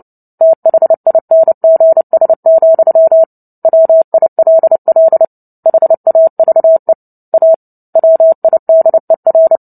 Morse code, heard as a keyed tone, about 30 wpm, 650 Hz. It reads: THINGS, WILL HAVE A WIDER
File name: Book Two / 03-Chapter 3 - 29 WPM